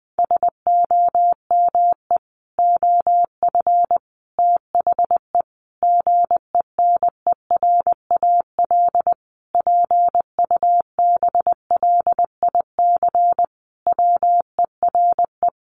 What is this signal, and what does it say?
Morse code, heard as a keyed tone, about 20 wpm, 700 Hz. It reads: SOME OF THE GENERAL PUBLIC WERE
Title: Sentences from Top 800 Words 20wpm